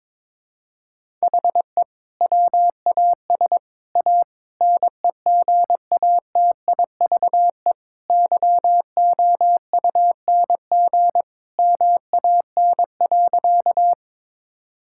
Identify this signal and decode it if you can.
Morse code, heard as a keyed tone, about 22 wpm, 700 Hz. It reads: HE WAS A NEGATIVE YOUNG MAN.
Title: Sentences from Top 1000 Words - Review Part 6 - 22wpm